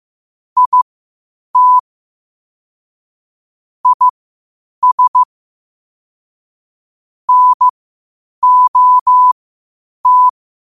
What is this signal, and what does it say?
Morse code, heard as a keyed tone, about 15 wpm, spaced as for 8 wpm, 1000 Hz. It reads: IT IS NOT
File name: 8wpm